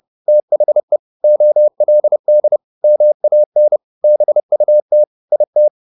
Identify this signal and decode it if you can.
Morse code, heard as a keyed tone, about 30 wpm, 600 Hz. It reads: THE OLD MAN BUT IT
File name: the old man and the sea / the old man and the sea 1 30wpm